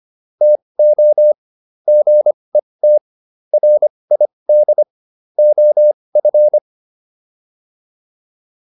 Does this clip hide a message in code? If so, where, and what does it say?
Yes: Morse code from 0.4 to 6.6 s, heard as a keyed tone, about 25 wpm, spaced as for 20 wpm, 600 Hz. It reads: TO GET RID OF